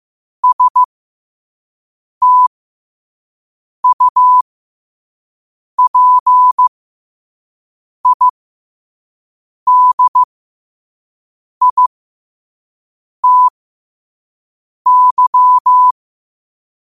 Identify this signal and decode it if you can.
Morse code, heard as a keyed tone, about 15 wpm, spaced as for 5 wpm, 1000 Hz. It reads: STUPIDITY